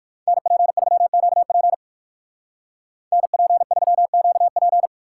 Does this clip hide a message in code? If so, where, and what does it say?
Yes: Morse code from 0.3 to 4.9 s, heard as a keyed tone, about 59 wpm, 700 Hz. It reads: NP3QP NP3QP